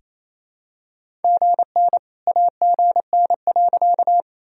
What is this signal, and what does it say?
Morse code, heard as a keyed tone, about 27 wpm, 700 Hz. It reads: GN AGN.
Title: Sentences from Top 300 Words and Abbreviations 28wpm